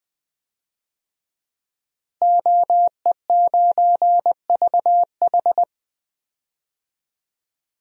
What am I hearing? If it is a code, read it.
Morse code, heard as a keyed tone, about 20 wpm, 700 Hz. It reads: OE9VH
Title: DXCC-100 Call Signs - 5 Characters - Repeated 3x - 20wpm